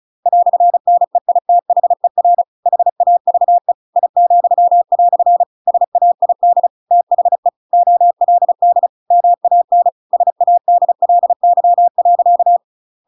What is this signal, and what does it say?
Morse code, heard as a keyed tone, about 34 wpm, 700 Hz. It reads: "NEITHER HAVE I," SAID THE OLD MAN SADLY.
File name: Pinocchio / Pinocchio ch8 35 35 ik1hge